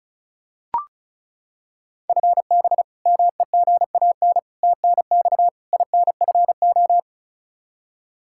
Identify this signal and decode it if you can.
Morse code, heard as a keyed tone, about 36 wpm, 700 Hz. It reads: FB MEGAN TNX INFO
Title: Practice QSO - Exchange 3 - More Info - 35wpm